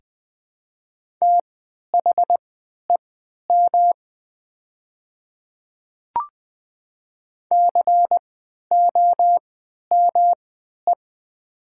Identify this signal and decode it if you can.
Morse code, heard as a keyed tone, about 20 wpm, spaced as for 11 wpm, 700 Hz. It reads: THEM COME
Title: 3x Character Spacing - Top 200 Words - 20wpm